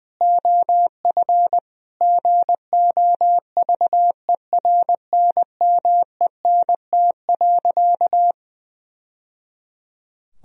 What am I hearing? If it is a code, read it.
Morse code, heard as a keyed tone, about 21 wpm, 700 Hz. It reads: OF GOVERNMENT.
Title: Sentences from Top 1000 Words - Review Part 3 - 20wpm